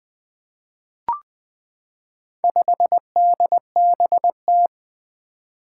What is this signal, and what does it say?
Morse code, heard as a keyed tone, about 20 wpm, 700 Hz. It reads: 5DBT
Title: Four Challenging Letter-Number Combinations 20wpm